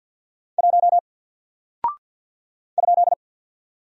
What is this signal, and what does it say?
Morse code, heard as a keyed tone, about 50 wpm, 700 Hz. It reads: 1 ?
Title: Single Character (Letter:Number:Punctuation:Prosign) 50wpm